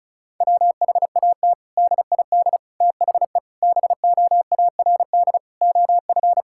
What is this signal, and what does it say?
Morse code, heard as a keyed tone, about 34 wpm, 700 Hz. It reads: WHAT DID THE BOARD OF
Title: Sentences from Top 1000 Words - Review Part 6 - 35wpm